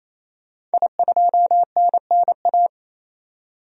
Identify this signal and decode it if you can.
Morse code, heard as a keyed tone, about 28 wpm, 700 Hz. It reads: I2NNA
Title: DXCC-100 Call Signs - 5 Characters - 28wpm